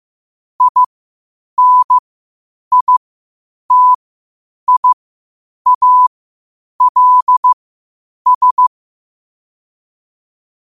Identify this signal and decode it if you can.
Morse code, heard as a keyed tone, about 15 wpm, spaced as for 8 wpm, 1000 Hz. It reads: INITIALS
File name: qotdfiles / 8wpm